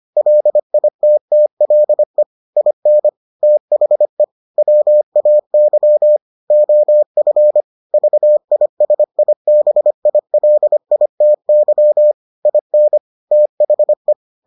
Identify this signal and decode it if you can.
Morse code, heard as a keyed tone, about 25 wpm, 600 Hz. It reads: LITTLE IN THE WAY OF VISIBILITY IN THE